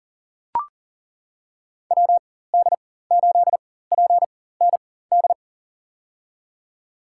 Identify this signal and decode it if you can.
Morse code, heard as a keyed tone, about 40 wpm, spaced as for 19 wpm, 700 Hz. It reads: WD8PND
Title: USA Call Signs - 6 Characters - 4x Character Spacing - 40wpm